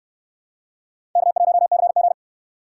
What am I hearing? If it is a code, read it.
Morse code, heard as a keyed tone, about 59 wpm, 700 Hz. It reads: K1CG